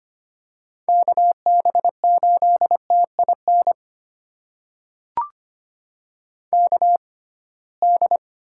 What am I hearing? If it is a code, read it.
Morse code, heard as a keyed tone, about 25 wpm, spaced as for 19 wpm, 700 Hz. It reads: KB8TIN KD
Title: USA Call Signs - 6 Characters - 6x Character Spacing - 25wpm